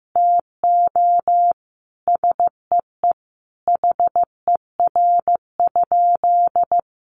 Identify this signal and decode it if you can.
Morse code, heard as a keyed tone, about 15 wpm, 700 Hz. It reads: TO SEE HER?
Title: Sentences from Top 1500 Words 15wpm